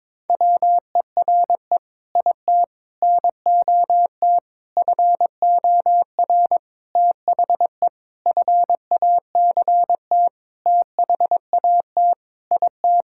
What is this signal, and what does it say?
Morse code, heard as a keyed tone, about 22 wpm, 700 Hz. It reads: WERE IT NOT FOR THE FACT THAT IT
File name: Sentences with 7-Word Phrases - 22wpm